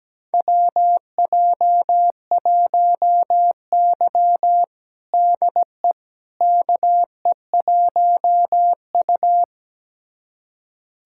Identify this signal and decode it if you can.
Morse code, heard as a keyed tone, about 17 wpm, 700 Hz. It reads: WJ1Y DE KE1U
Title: USA Callsign DE Callsign - 4 Characters - 17wpm